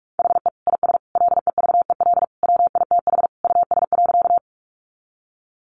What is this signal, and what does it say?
Morse code, heard as a keyed tone, about 44 wpm, 700 Hz. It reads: HE IS LEVEL WITH US.
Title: Sentences from Top 700 Words 45wpm